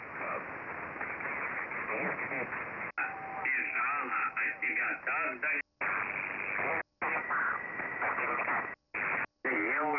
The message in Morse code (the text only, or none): none